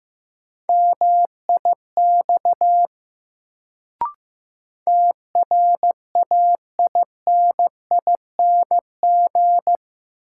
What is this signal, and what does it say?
Morse code, heard as a keyed tone, about 15 wpm, 700 Hz. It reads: MIX TRAINING
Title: Sentences from Top 1000 Words - Review Part 4 - New Word Drill - 15wpm